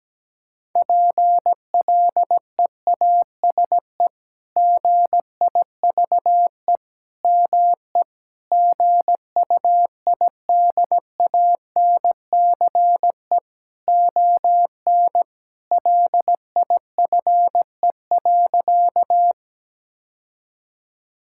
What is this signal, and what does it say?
Morse code, heard as a keyed tone, about 17 wpm, 700 Hz. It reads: PLEASE GIVE ME GUIDANCE ON LIFE.
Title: Sentences from Top 1600 Words 17wpm